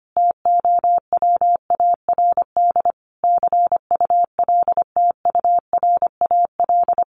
Morse code, heard as a keyed tone, about 25 wpm, 700 Hz. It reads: TOWARD CULTURAL
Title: Sentences from Top 1500 Words 25wpm